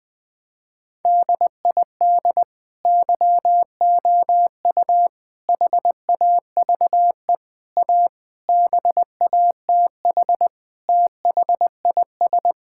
Morse code, heard as a keyed tone, about 21 wpm, 700 Hz. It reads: DID YOU HAVE A BATH THIS
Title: Sentences from Top 1600 Words 20wpm